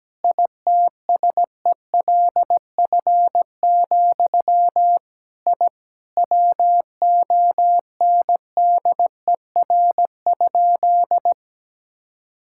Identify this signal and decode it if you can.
Morse code, heard as a keyed tone, about 17 wpm, 700 Hz. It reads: ITSELF, I WONDER?
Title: Sentences from Top 1000 Words - Review Part 5 - 17wpm